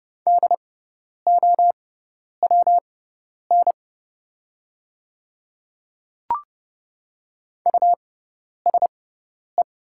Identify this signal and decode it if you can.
Morse code, heard as a keyed tone, about 30 wpm, spaced as for 10 wpm, 700 Hz. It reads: DOWN USE
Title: 6x Character Spacing - Top 100 Words - 30wpm